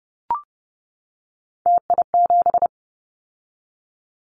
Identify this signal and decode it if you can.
Morse code, heard as a keyed tone, about 30 wpm, 700 Hz. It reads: TI7
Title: Callsign Prefixes for Top 100 DXCC - 30wpm